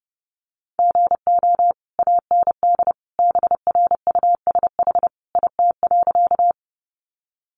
Morse code, heard as a keyed tone, about 28 wpm, 700 Hz. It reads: GO AND BRUSH IT.
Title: Sentences from Top 1100 Words 30wpm